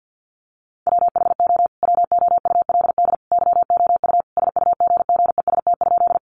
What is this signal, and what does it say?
Morse code, heard as a keyed tone, about 48 wpm, 700 Hz. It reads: WHO WOULD YOU SUGGEST?
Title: Sentences from Top 1300 Words 50wpm